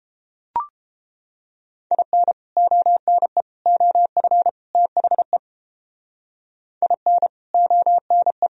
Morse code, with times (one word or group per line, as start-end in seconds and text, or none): 1.9-2.3 IN
2.6-3.4 ONE
3.7-4.5 OF
4.7-5.4 THE
6.8-7.3 IN
7.5-8.5 ONE